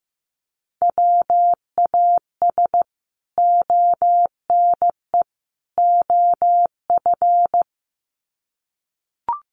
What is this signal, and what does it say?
Morse code, heard as a keyed tone, about 15 wpm, 700 Hz. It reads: WAS ONE OF
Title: Sets of 3 Words 15wpm